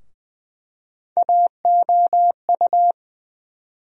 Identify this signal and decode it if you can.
Morse code, heard as a keyed tone, about 21 wpm, 700 Hz. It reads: AOU
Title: USA Call Sign Postfixes - 3 Letters 20wpm